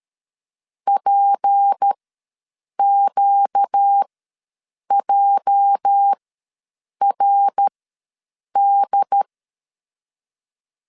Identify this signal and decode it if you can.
Morse code, heard as a keyed tone, about 13 wpm, spaced as for 7 wpm, 800 Hz. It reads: PQJRD
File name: LEZIONI / 9 28 ws 12wpm 3354